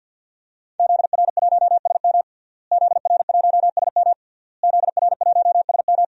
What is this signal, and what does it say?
Morse code, heard as a keyed tone, about 51 wpm, 700 Hz. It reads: ZR1SM ZR1SM ZR1SM
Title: DXCC-100 Call Signs - 5 Characters - Repeated 3x - 50wpm